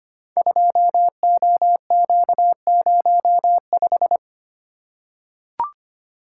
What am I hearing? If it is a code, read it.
Morse code, heard as a keyed tone, about 25 wpm, 700 Hz. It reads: 2OQ05
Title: Code Groups - 5 Letter:Numbers - 25wpm